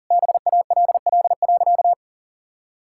Morse code, heard as a keyed tone, about 40 wpm, 700 Hz. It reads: BALL.
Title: Sentences from Top 1000 Words - Review Part 6 - 40wpm